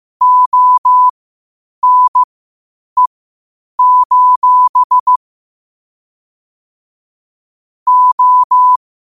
Morse code, heard as a keyed tone, about 15 wpm, spaced as for 8 wpm, 1000 Hz. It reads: ONE: O
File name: qotdfiles / 8wpm